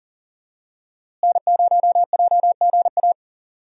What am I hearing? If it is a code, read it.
Morse code, heard as a keyed tone, about 40 wpm, 700 Hz. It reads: N0JGA
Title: USA Call Signs - 5 Characters - Repeated 3x - 40wpm